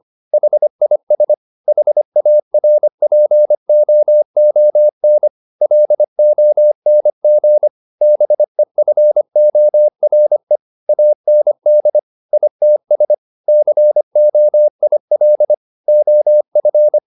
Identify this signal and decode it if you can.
Morse code, heard as a keyed tone, about 25 wpm, 600 Hz. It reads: HIS HARPOON LONG BEFORE AND ITS COIL OF